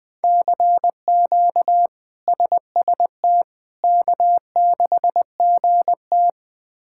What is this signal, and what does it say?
Morse code, heard as a keyed tone, about 20 wpm, 700 Hz. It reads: CQ SST K6GT
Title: K1USN SST QSO Practice - 20wpm